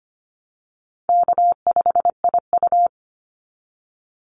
Morse code, heard as a keyed tone, about 25 wpm, 700 Hz. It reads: K5IU
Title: Easy Call Signs Repeated 3x - 25wpm